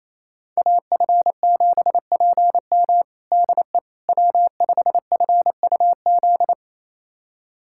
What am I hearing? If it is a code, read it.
Morse code, heard as a keyed tone, about 27 wpm, 700 Hz. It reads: AF7PM DE W5FUZ
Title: USA Callsign DE Callsign - 5 Characters - 28wpm